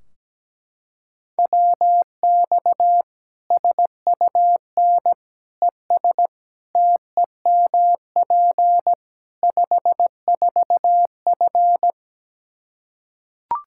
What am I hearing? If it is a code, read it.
Morse code, heard as a keyed tone, about 17 wpm, 700 Hz. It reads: WX SUN ES TEMP 54F
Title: Practice QSO - Exchange 2 - Info - 17wpm